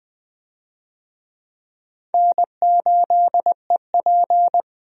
Morse code, heard as a keyed tone, about 20 wpm, 700 Hz. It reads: N8EP
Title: USA Call Signs - 4 Characters - 20wpm